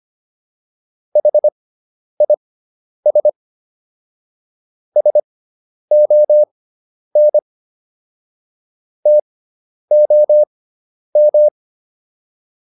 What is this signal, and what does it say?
Morse code, heard as a keyed tone, about 25 wpm, spaced as for 10 wpm, 600 Hz. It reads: HIS SON TOM